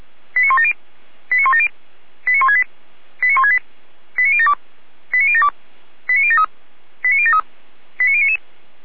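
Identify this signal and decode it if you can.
Five-tone selective call (ZVEI), heard as quick runs of five tones, tones of about 70 ms. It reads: 89170 89170 89168 89168 89072 89072 89073 89073 89099
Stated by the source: ZVEI five-tone selective call received on SDR.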